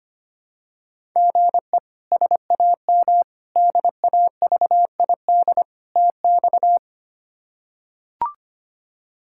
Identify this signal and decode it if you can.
Morse code, heard as a keyed tone, about 25 wpm, 700 Hz. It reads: GE SAM DAVID TX
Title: K1USN SST QSO Practice - 25wpm